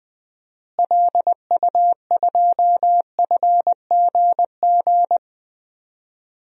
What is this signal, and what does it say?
Morse code, heard as a keyed tone, about 20 wpm, 700 Hz. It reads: LU2FGG
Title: DXCC-100 Call Signs - 6 Characters - Repeated 2x - 20wpm